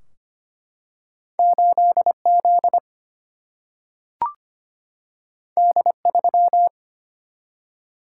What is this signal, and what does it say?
Morse code, heard as a keyed tone, about 25 wpm, 700 Hz. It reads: 8Z D3
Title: Single-Letter Single-Number 25wpm